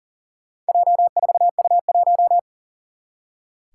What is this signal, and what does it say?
Morse code, heard as a keyed tone, about 38 wpm, 700 Hz. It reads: J4U1
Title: Four Challenging Letter-Number Combinations 40wpm